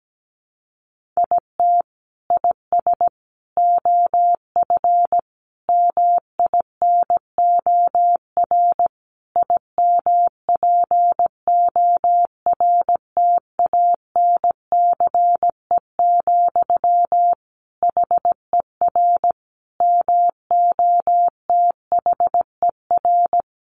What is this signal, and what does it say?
Morse code, heard as a keyed tone, about 17 wpm, 700 Hz. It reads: IT IS OF MINOR IMPORTANCE, HER MOTHER
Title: Sentences from Top 1500 Words 17wpm